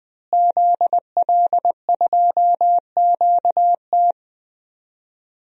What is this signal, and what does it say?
Morse code, heard as a keyed tone, about 20 wpm, 700 Hz. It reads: ZL2QT
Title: DXCC-100 Call Signs - 5 Characters - Repeated 3x - 20wpm